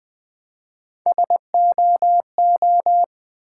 Morse code, heard as a keyed tone, about 20 wpm, 700 Hz. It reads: SOO
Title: Build A Word - 1 Letter At A Time - 4-Letter Words - 20wpm